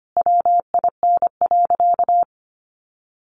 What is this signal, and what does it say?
Morse code, heard as a keyed tone, about 25 wpm, 700 Hz. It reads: WIN.
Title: Sentences from Top 1500 Words 25wpm